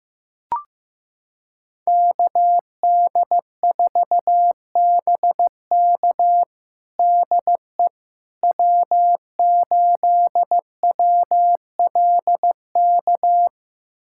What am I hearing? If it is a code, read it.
Morse code, heard as a keyed tone, about 15 wpm, 700 Hz. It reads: KD4BK DE W8WLK
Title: USA Callsign DE Callsign 15wpm